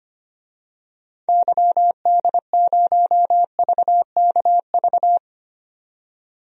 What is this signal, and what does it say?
Morse code, heard as a keyed tone, about 25 wpm, 700 Hz. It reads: YD0VKV
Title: DXCC-100 Call Signs - 6 Characters - 25wpm